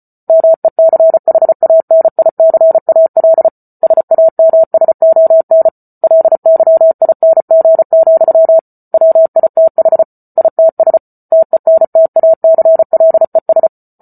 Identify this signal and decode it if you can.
Morse code, heard as a keyed tone, about 34 wpm, 650 Hz. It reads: MECHANICAL SAMSON LYING, WITH ITS TENTACLES